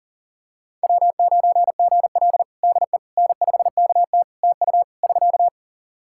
Morse code, heard as a keyed tone, about 40 wpm, 700 Hz. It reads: W9GL DE N5KT TU <SK>
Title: Practice QSO - Exchange 4 - Ending - 40wpm